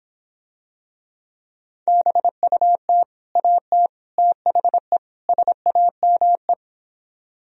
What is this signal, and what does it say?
Morse code, heard as a keyed tone, about 26 wpm, 700 Hz. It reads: BUT AT THE SAME